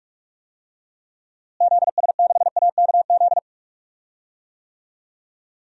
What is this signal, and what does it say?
Morse code, heard as a keyed tone, about 46 wpm, 700 Hz. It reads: ZS6AKZ
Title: DXCC-100 Call Signs - 6 Characters - Repeated 3x - 45wpm